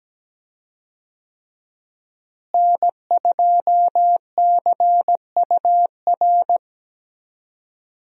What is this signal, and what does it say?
Morse code, heard as a keyed tone, about 17 wpm, 700 Hz. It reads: N2CUR
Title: USA Call Signs - 5 Characters - Repeated 3x - 17wpm